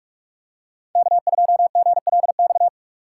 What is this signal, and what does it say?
Morse code, heard as a keyed tone, about 41 wpm, 700 Hz. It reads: K2GLX
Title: USA Call Signs - 5 Characters - 2x Character Spacing - 45wpm